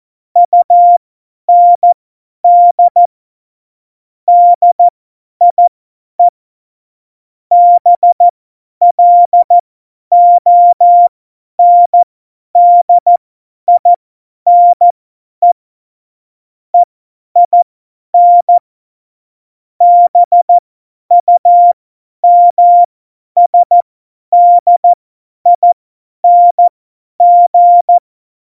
Morse code, heard as a keyed tone, about 14 wpm, spaced as for 10 wpm, 700 Hz. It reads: UND DIE BLONDINE EIN BUMSDING
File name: Morsekurs / Witze / Witz 18 Juli 17, Tempo 50 F70